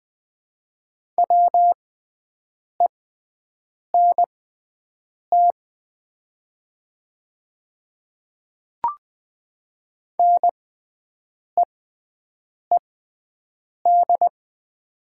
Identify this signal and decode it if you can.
Morse code, heard as a keyed tone, about 20 wpm, spaced as for 7 wpm, 700 Hz. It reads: WENT NEED
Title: 6x Character Spacing - Top 500 Words - 20wpm